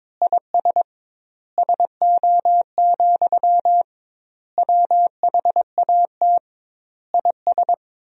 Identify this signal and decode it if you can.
Morse code, heard as a keyed tone, about 22 wpm, 700 Hz. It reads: IS SO, WHAT IS
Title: 2x Word-Spacing - Sentences from Top 200 Words - 22wpm